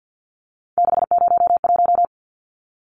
Morse code, heard as a keyed tone, about 50 wpm, 700 Hz. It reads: 601
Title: Numbers - 3-Digit 50wpm